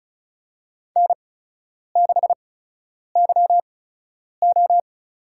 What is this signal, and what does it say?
Morse code, heard as a keyed tone, about 36 wpm, spaced as for 10 wpm, 700 Hz. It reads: N6YO
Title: USA Call Signs - 8x Character Spacing - 35wpm